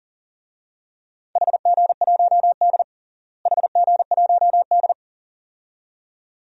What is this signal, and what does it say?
Morse code, heard as a keyed tone, about 40 wpm, 700 Hz. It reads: HG1D HG1D
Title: DXCC-100 Call Signs - 4 Characters - Repeated 2x - 40wpm